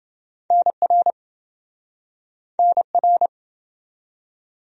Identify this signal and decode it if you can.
Morse code, heard as a keyed tone, about 27 wpm, 700 Hz. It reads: NR NR